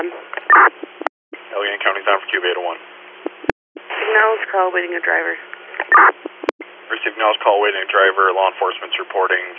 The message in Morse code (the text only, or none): none